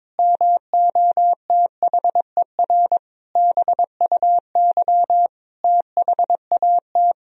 Morse code, heard as a keyed tone, about 22 wpm, 700 Hz. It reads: MOTHER BUY THAT
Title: Sentences from Top 1000 Words - Review Part 3 - 22wpm